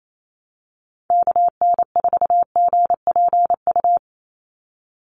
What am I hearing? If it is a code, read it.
Morse code, heard as a keyed tone, about 28 wpm, 700 Hz. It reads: KN4GPU
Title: USA Call Signs 28wpm